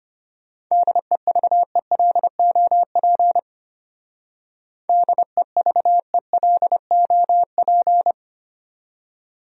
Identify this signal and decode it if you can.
Morse code, heard as a keyed tone, about 32 wpm, 700 Hz. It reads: DEVELOP DEVELOP